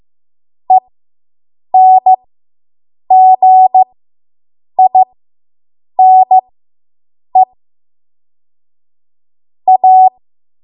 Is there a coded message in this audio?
Yes: Morse code from 0.7 to 10.1 s, heard as a keyed tone, about 15 wpm, spaced as for 7 wpm, 750 Hz. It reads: ENGINE A